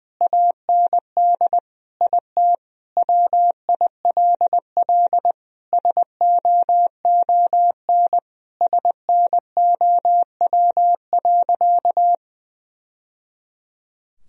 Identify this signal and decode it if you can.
Morse code, heard as a keyed tone, about 20 wpm, 700 Hz. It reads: AND IT WILL SOON SNOW.
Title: Sentences from Top 1000 Words - Review Part 3 - 20wpm